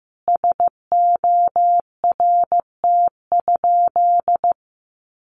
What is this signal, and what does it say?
Morse code, heard as a keyed tone, about 15 wpm, 700 Hz. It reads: SORT?
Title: Sentences from Top 1300 Words 15wpm